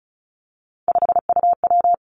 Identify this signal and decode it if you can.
Morse code, heard as a keyed tone, about 36 wpm, 700 Hz. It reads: 5UW
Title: Three Challenging Letter-Number Combinations 35wpm